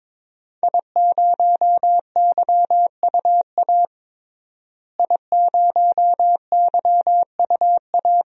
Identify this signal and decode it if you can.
Morse code, heard as a keyed tone, about 22 wpm, 700 Hz. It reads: I0YUA I0YUA
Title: DXCC-100 Call Signs - 5 Characters - Repeated 3x - 22wpm